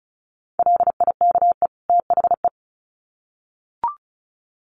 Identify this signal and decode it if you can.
Morse code, heard as a keyed tone, about 34 wpm, 700 Hz. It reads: LIKE THE
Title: Sets of 2 Words 35wpm